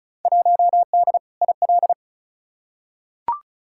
Morse code, heard as a keyed tone, about 36 wpm, 700 Hz. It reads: 1D IL
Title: Field Day Exchanges - 35wpm